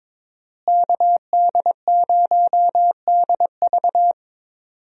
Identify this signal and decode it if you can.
Morse code, heard as a keyed tone, about 22 wpm, 700 Hz. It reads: KD0DV
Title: USA Call Signs - 5 Characters - Repeated 3x - 22wpm